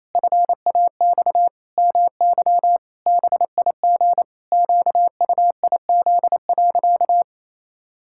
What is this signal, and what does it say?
Morse code, heard as a keyed tone, about 28 wpm, 700 Hz. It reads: FAX MY BIG QUIZ.
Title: Pangram Challenge v2 - 28wpm